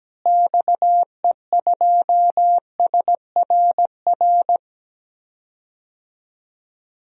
Morse code, heard as a keyed tone, about 17 wpm, 700 Hz. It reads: XE2SRR
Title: DXCC-100 Call Signs - 6 Characters - Repeated 3x - 17wpm